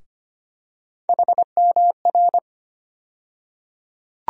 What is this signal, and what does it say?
Morse code, heard as a keyed tone, about 25 wpm, 700 Hz. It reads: HMR
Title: USA Call Sign Postfixes - 3 Letters 25wpm